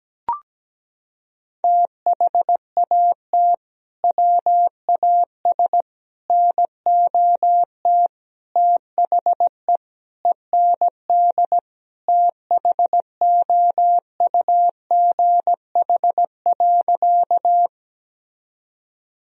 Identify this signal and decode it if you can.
Morse code, heard as a keyed tone, about 17 wpm, 700 Hz. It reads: THAT WAS NOT THE END THOUGH.
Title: Sentences from Top 1000 Words - Review Part 2 - 17wpm